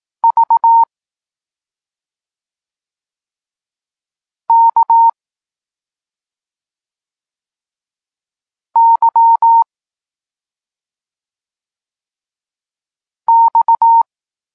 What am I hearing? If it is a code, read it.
Morse code, heard as a keyed tone, about 18 wpm, 920 Hz. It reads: V K Y X
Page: Tekst 32 – Doskonalenie odbioru poznanych liter